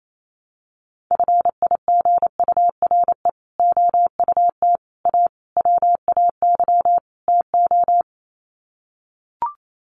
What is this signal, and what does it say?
Morse code, heard as a keyed tone, about 28 wpm, 700 Hz. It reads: FIGURE OUT A WAY TO
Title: Sets of 5 Words Encore2 28wpm